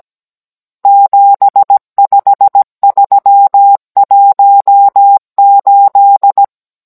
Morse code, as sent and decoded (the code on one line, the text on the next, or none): --... ..... ...-- .---- ---..
75318